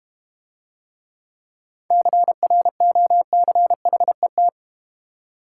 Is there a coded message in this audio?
Yes: Morse code from 1.9 to 4.5 s, heard as a keyed tone, about 32 wpm, 700 Hz. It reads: CROCHET